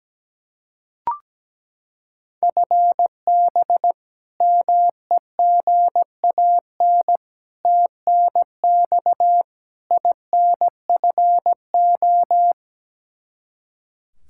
Morse code, heard as a keyed tone, about 17 wpm, 700 Hz. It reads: FB MEGAN TNX INFO
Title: Practice QSO - Exchange 3 - More Info - 17wpm